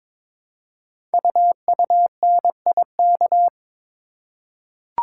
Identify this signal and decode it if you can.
Morse code, heard as a keyed tone, about 22 wpm, 700 Hz. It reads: UUNIK
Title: Code Groups - 5 Letters - 22wpm